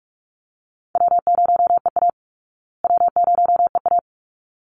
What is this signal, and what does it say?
Morse code, heard as a keyed tone, about 44 wpm, 700 Hz. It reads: W0EA W0EA
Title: Call Signs Repeated 3x - 45wpm